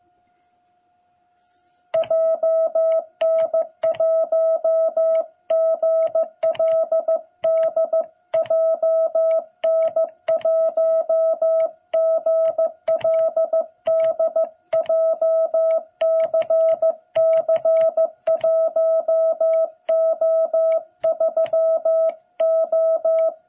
Morse code, heard as a keyed tone, about 15 wpm, 660 Hz. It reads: JN1GLD JN1GLD JCC1O3O
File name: jh8cbh / 2021cw